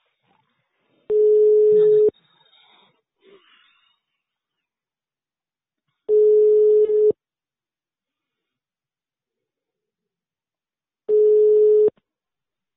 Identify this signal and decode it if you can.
Ringback tone (Europe, 425 Hz).